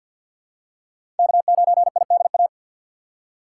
Morse code, heard as a keyed tone, about 51 wpm, 700 Hz. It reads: K9IDA
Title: USA Call Signs - 5 Characters - 2x Character Spacing - 50wpm